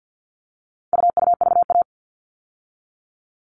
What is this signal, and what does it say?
Morse code, heard as a keyed tone, about 50 wpm, 700 Hz. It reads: UUVA
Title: All Cut Numbers - 4-Digit 50wpm